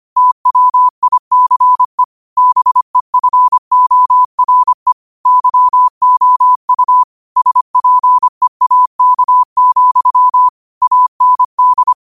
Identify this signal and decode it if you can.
Morse code, heard as a keyed tone, about 25 wpm, 1000 Hz. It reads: TWICE BEFORE YOU SPEAK, AND
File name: qotdfiles / 25wpm